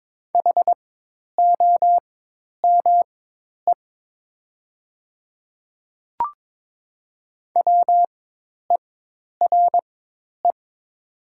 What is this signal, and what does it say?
Morse code, heard as a keyed tone, about 22 wpm, spaced as for 10 wpm, 690 Hz. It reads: HOME WERE
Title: 4x Character Spacing - Top 200 Words - 22wpm